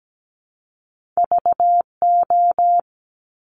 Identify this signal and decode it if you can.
Morse code, heard as a keyed tone, about 17 wpm, 700 Hz. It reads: VO
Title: Top Two Letter Combinations in Words 17wpm